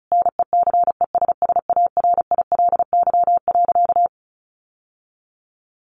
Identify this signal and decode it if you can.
Morse code, heard as a keyed tone, about 34 wpm, 700 Hz. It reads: NECESSARILY.
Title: Sentences from Top 1400 Words 35wpm